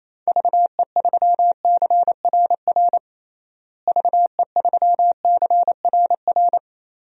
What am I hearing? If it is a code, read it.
Morse code, heard as a keyed tone, about 28 wpm, 700 Hz. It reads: VE3CRR VE3CRR
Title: DXCC-100 Call Signs - 6 Characters - Repeated 3x - 28wpm